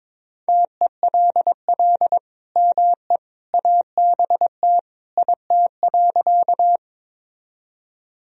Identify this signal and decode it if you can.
Morse code, heard as a keyed tone, about 22 wpm, 700 Hz. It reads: TELL ME ABT IT.
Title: Sentences from Top 200 Words and Abbreviations 22wpm